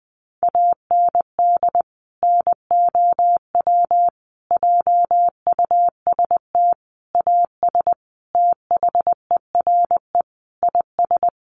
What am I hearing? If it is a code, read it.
Morse code, heard as a keyed tone, about 20 wpm, 700 Hz. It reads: AND NOW JUST AS THERE IS
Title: Sentences from Top 1200 Words 20wpm